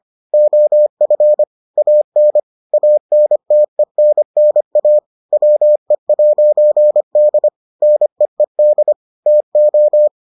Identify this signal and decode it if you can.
Morse code, heard as a keyed tone, about 25 wpm, 600 Hz. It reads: OF AN ANTENNA WE'D NEED TO